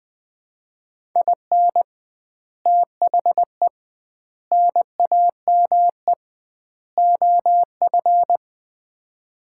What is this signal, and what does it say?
Morse code, heard as a keyed tone, about 20 wpm, 700 Hz. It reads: IN THE NAME OF
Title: 2x Word-Spacing - Sets of 4 Words - 20wpm